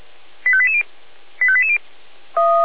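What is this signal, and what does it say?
Five-tone selective call (ZVEI), heard as quick runs of five tones, tones of about 80 ms. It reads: 85880 85880
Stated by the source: ZVEI five-tone call received by SDR.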